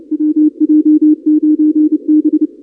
Morse code, heard as a keyed tone, about 30 wpm, 300 Hz. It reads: WJ9B
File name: contests / 2019-cw-wpx-kt5j-k5pi- -2330 0-31